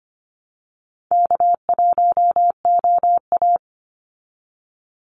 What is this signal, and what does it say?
Morse code, heard as a keyed tone, about 25 wpm, 700 Hz. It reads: K1OA
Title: Call Signs Repeated 2x - 25wpm